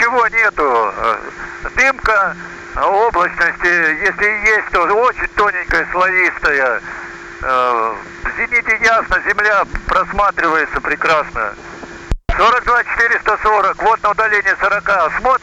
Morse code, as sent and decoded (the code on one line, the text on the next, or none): none